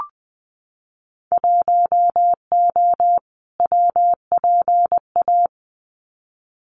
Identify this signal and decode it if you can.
Morse code, heard as a keyed tone, about 20 wpm, 700 Hz. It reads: 1O WPA
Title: Winter Field Day Exchanges 20wpm